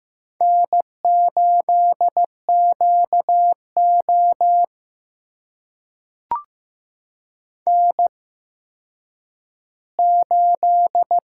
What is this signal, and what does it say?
Morse code, heard as a keyed tone, about 15 wpm, spaced as for 9 wpm, 700 Hz. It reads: N8QO N8
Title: USA Call Signs - 4 Characters - 8x Character Spacing - 15wpm